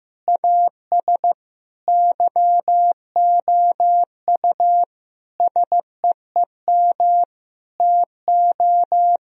Morse code, heard as a keyed tone, about 15 wpm, 700 Hz. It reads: AS YOU SEEM TO
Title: Sentences from Top 1000 Words - Review Part 1 - 15wpm